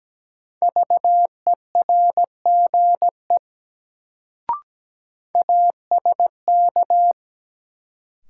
Morse code, heard as a keyed tone, about 17 wpm, 700 Hz. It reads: VERGE ASK
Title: Sets of 4 Words - New Word Drill - 17wpm